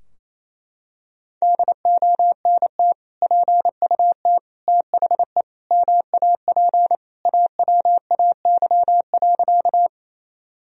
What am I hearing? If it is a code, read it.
Morse code, heard as a keyed tone, about 28 wpm, 700 Hz. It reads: DONT PUT THE MAP AWAY.
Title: Sentences from Top 1000 Words - Review Part 2 - 28wpm